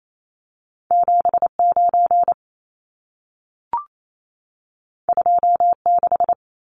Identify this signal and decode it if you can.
Morse code, heard as a keyed tone, about 28 wpm, 700 Hz. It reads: 79 26
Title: Numbers - 2-Digit 28wpm